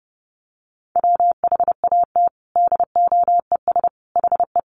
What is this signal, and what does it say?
Morse code, heard as a keyed tone, about 30 wpm, 700 Hz. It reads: WHAT DOES HE
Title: Sentences from Top 200 Words 30wpm